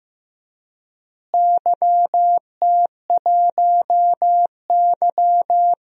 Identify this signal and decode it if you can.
Morse code, heard as a keyed tone, about 16 wpm, 700 Hz. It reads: YT1Y
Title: DXCC-100 Call Signs - 4 Characters - Repeated 2x - 15wpm